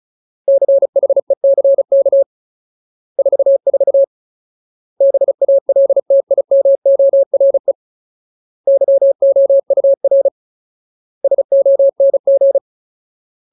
Morse code, heard as a keyed tone, about 34 wpm, 550 Hz. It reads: CHECK 44 BALTIMORE YOUR SONG
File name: Ultimate Word List / 35WPM Ultimate